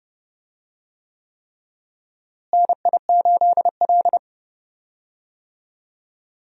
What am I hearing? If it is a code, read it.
Morse code, heard as a keyed tone, about 29 wpm, 700 Hz. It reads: NI8L
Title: USA Call Signs - 4 Characters - Repeated 3x - 30wpm